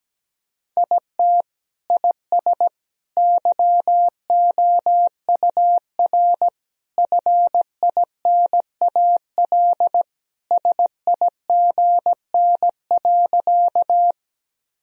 Morse code, heard as a keyed tone, about 17 wpm, 700 Hz. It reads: IT IS YOUR FINAL SIGN.
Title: Sentences from Top 1000 Words - Review Part 2 - 17wpm